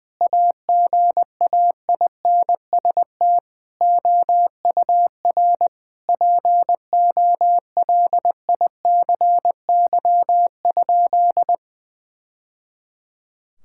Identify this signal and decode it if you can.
Morse code, heard as a keyed tone, about 20 wpm, 700 Hz. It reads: AGAINST OUR POLICY?
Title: Sentences from Top 1000 Words - Review Part 5 - 20wpm